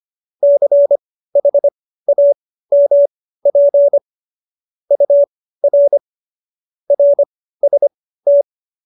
Morse code, heard as a keyed tone, about 25 wpm, spaced as for 15 wpm, 570 Hz. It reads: CHAMP UR RST